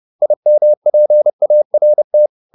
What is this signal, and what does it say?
Morse code, heard as a keyed tone, about 30 wpm, 600 Hz. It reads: IMPART